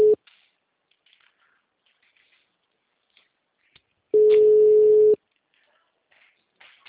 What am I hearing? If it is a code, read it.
Ringback tone (Europe, 425 Hz).